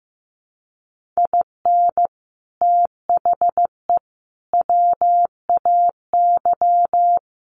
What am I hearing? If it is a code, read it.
Morse code, heard as a keyed tone, about 15 wpm, 700 Hz. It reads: IN THE WAY
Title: Sets of 3 Words 15wpm